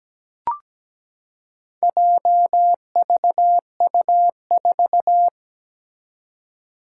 Morse code, heard as a keyed tone, about 17 wpm, 700 Hz. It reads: JVU4
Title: Four Challenging Letter-Number Combinations 17wpm